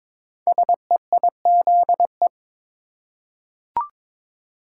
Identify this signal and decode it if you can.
Morse code, heard as a keyed tone, about 22 wpm, 700 Hz. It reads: SEIZE
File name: 5 Letter Words - First 3 Letters Given - 22wpm